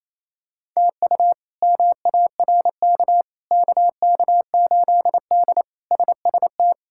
Morse code, heard as a keyed tone, about 28 wpm, 700 Hz. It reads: TU MARK KK8D SST
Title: K1USN SST QSO Practice - 28wpm